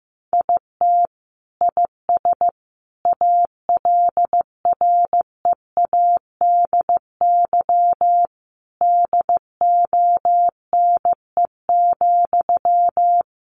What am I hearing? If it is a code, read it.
Morse code, heard as a keyed tone, about 15 wpm, 700 Hz. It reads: IT IS ALREADY DONE,
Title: Sentences from Top 700 Words 15wpm